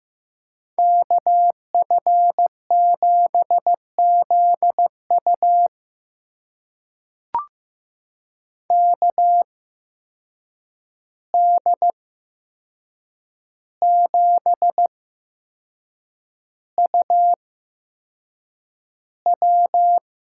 Morse code, heard as a keyed tone, about 15 wpm, spaced as for 6 wpm, 700 Hz. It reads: KF7ZU KD7UW
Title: USA Call Signs - 5 Characters - 8x Character Spacing - 15wpm